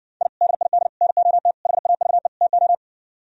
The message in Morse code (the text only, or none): I DID NOT HAVE TO